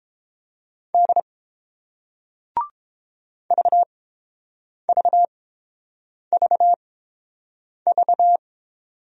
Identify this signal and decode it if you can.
Morse code, heard as a keyed tone, about 30 wpm, spaced as for 7 wpm, 700 Hz. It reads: D VVVV